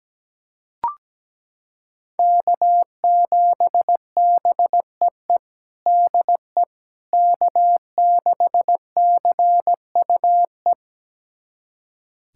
Morse code, heard as a keyed tone, about 17 wpm, 700 Hz. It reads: K7BEE DE K6CUE
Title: USA Callsign DE Callsign - 5 Characters - 17wpm